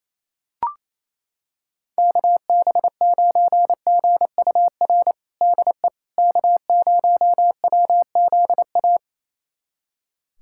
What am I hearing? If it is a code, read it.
Morse code, heard as a keyed tone, about 28 wpm, 700 Hz. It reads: KB9GUR DE K0WZA
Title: USA Callsign DE Callsign 28wpm